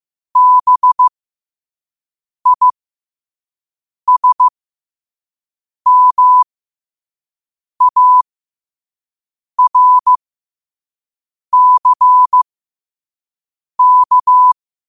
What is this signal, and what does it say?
Morse code, heard as a keyed tone, about 15 wpm, spaced as for 5 wpm, 1000 Hz. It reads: BISMARCK
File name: qotdfiles / 5wpm